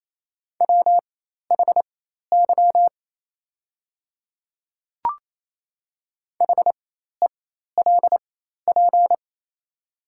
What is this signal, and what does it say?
Morse code, heard as a keyed tone, about 27 wpm, spaced as for 13 wpm, 690 Hz. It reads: WHY HELP
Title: 4x Character Spacing - Top 200 Words - 28wpm